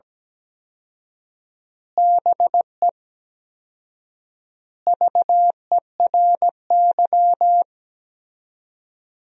Morse code, heard as a keyed tone, about 17 wpm, 700 Hz. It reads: BE VERY
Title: 4x Word-Spacing - Sentences from Top 200 Words - 17wpm